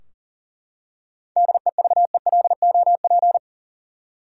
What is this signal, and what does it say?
Morse code, heard as a keyed tone, about 38 wpm, 700 Hz. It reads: DEVELOP
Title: Seven Letter Words - 40wpm